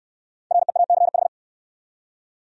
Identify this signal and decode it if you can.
Morse code, heard as a keyed tone, about 65 wpm, 700 Hz. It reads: REACH